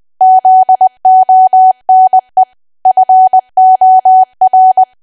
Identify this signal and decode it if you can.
Morse code, heard as a keyed tone, about 20 wpm, 750 Hz. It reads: ZONE FOR
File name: cwa / W1AW-Level3 / 20WPM